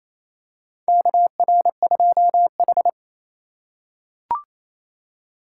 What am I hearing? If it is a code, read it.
Morse code, heard as a keyed tone, about 28 wpm, 700 Hz. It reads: KR2H
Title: USA Call Signs - 4x Character Spacing - 28wpm